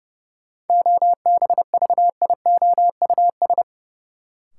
Morse code, heard as a keyed tone, about 30 wpm, 700 Hz. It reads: OBVIOUS
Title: Sentences from Top 1600 Words - New Word Drill - 30wpm